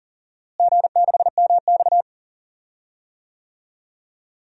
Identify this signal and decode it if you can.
Morse code, heard as a keyed tone, about 40 wpm, 700 Hz. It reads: G6MX
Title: DXCC-100 Call Signs - 4 Characters - Repeated 3x - 40wpm